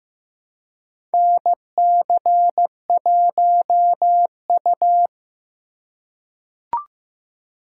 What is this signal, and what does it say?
Morse code, heard as a keyed tone, about 15 wpm, 700 Hz. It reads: NC1U
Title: USA Call Signs - 4 Characters - 2x Character Spacing - 15wpm